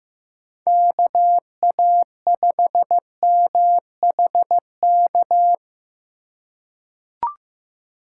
Morse code, heard as a keyed tone, about 16 wpm, 700 Hz. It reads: KA5MHK
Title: USA Call Signs - 6 Characters - 4x Character Spacing - 15wpm